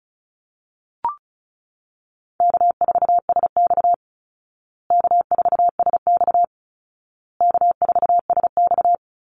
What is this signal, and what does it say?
Morse code, heard as a keyed tone, about 34 wpm, 700 Hz. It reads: K4SX K4SX K4SX
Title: Easy Call Signs Repeated 3x - 35wpm